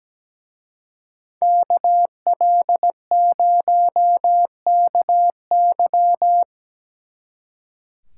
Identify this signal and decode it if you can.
Morse code, heard as a keyed tone, about 19 wpm, 700 Hz. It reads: KL0KY
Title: Speed Racing - USA 2x2 Call Signs 17wpm